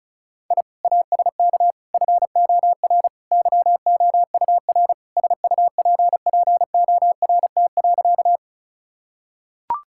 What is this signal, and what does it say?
Morse code, heard as a keyed tone, about 36 wpm, 700 Hz. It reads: I ASK FOR YOUR SUPPORT.
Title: Sentences from Top 1000 Words - Review Part 5 - 35wpm